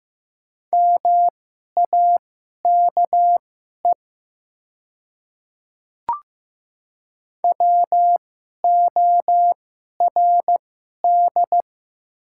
Morse code, heard as a keyed tone, about 15 wpm, spaced as for 11 wpm, 700 Hz. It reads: MAKE WORD
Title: 2x Character Spacing - Top 100 Words - 15wpm